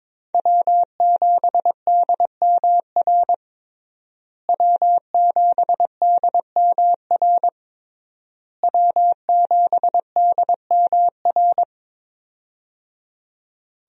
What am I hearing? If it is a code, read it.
Morse code, heard as a keyed tone, about 22 wpm, 700 Hz. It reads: W7DMR W7DMR W7DMR
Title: USA Call Signs - 5 Characters - Repeated 3x - 22wpm